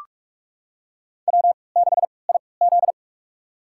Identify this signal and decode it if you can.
Morse code, heard as a keyed tone, about 44 wpm, spaced as for 25 wpm, 700 Hz. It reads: W6IZ
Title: USA Call Signs - 4 Characters - 3x Character Spacing - 45wpm